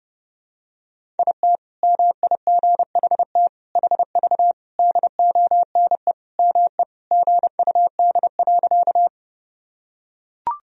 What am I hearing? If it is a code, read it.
Morse code, heard as a keyed tone, about 30 wpm, 700 Hz. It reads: IT MIGHT HV DONE ME GUD.
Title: Sentences from Top 500 Words and Abbreviations 30wpm